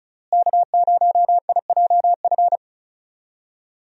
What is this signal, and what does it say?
Morse code, heard as a keyed tone, about 34 wpm, 700 Hz. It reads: K0IJF
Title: USA Call Signs - 5 Characters - Repeated 3x - 35wpm